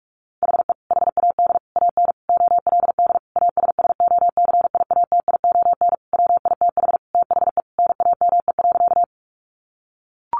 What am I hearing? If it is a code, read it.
Morse code, heard as a keyed tone, about 44 wpm, 700 Hz. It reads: HE HAD AN OLD ASSOCIATION WITH THE NAME.
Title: Sentences from Top 1100 Words 45wpm